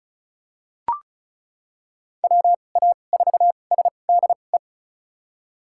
Morse code, heard as a keyed tone, about 34 wpm, spaced as for 25 wpm, 690 Hz. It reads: WA4SDE
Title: USA Call Signs - 6 Characters - 2x Character Spacing - 35wpm